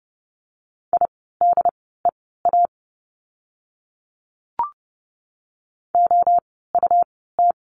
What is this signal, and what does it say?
Morse code, heard as a keyed tone, about 30 wpm, spaced as for 17 wpm, 700 Hz. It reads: IDEA OUT
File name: course / 30:10wpm / 108 Lesson 18f - Letter m - Words 30wpm